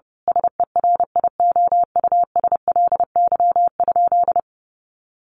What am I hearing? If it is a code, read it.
Morse code, heard as a keyed tone, about 30 wpm, 700 Hz. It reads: SERIOUSLY?
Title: Sentences from Top 1200 Words 30wpm